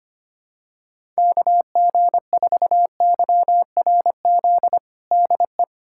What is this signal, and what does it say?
Morse code, heard as a keyed tone, about 25 wpm, 700 Hz. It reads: KG4YRZ DE
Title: USA Callsign DE Callsign - 6 Characters - 25wpm